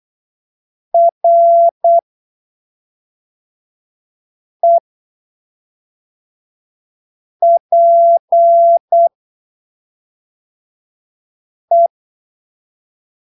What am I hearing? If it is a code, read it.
Morse code, heard as a keyed tone, about 8 wpm, spaced as for 3 wpm, 680 Hz. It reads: REPE
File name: arquivos / texto / texto 08 08ppm